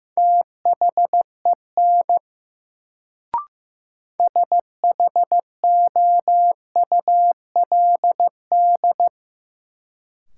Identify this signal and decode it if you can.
Morse code, heard as a keyed tone, about 15 wpm, 700 Hz. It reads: THEN SHOULD
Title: Sets of 2 Words - New Word Drill - 15wpm